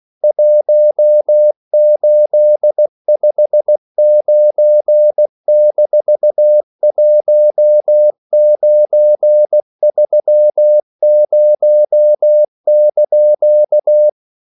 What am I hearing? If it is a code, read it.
Morse code, heard as a keyed tone, about 16 wpm, 600 Hz. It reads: 1859-1930)
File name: CW / Practice / EnglishQuotations0216